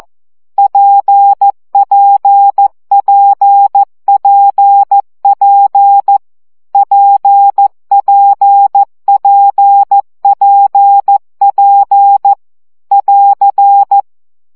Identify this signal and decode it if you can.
Morse code, heard as a keyed tone, about 14 wpm, 800 Hz. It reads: PPPPP PPPPP <AR>
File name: CW gyakorlas p 12 baud